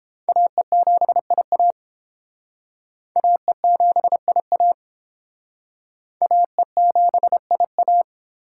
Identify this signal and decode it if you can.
Morse code, heard as a keyed tone, about 29 wpm, 700 Hz. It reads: AE7IA AE7IA AE7IA